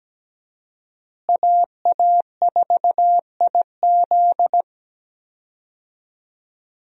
Morse code, heard as a keyed tone, about 17 wpm, 700 Hz. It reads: AA4IZ
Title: USA Call Signs - 5 Characters - Repeated 3x - 17wpm